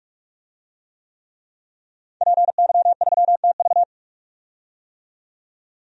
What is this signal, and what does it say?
Morse code, heard as a keyed tone, about 44 wpm, 700 Hz. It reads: PY3TV
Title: DXCC-100 Call Signs - 5 Characters - 45wpm